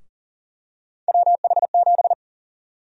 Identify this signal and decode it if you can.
Morse code, heard as a keyed tone, about 40 wpm, 700 Hz. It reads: WH7
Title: USA Call Sign Prefixes - Repeated 2x - 40wpm